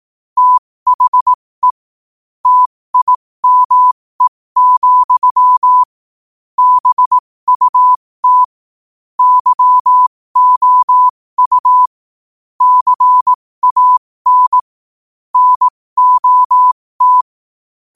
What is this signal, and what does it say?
Morse code, heard as a keyed tone, about 18 wpm, spaced as for 15 wpm, 1000 Hz. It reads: THE TIME, BUT YOU CAN NOT
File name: 15wpm